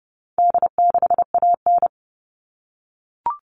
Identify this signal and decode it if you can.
Morse code, heard as a keyed tone, about 30 wpm, 700 Hz. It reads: D6AN
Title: All Cut Numbers - 4-Digit 30wpm